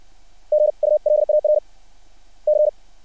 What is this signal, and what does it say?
Morse code, heard as a keyed tone, about 60 wpm, 600 Hz. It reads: GULAR C
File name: archived / CRSnet